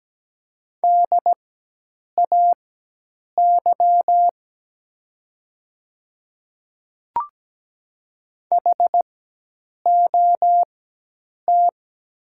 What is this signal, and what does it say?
Morse code, heard as a keyed tone, about 17 wpm, spaced as for 8 wpm, 700 Hz. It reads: DAY HOT
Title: 4x Character Spacing - Top 200 Words - 17wpm